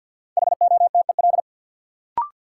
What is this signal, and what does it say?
Morse code, heard as a keyed tone, about 50 wpm, 700 Hz. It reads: HOTEL
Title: Sentences from Top 1600 Words - New Word Drill - 50wpm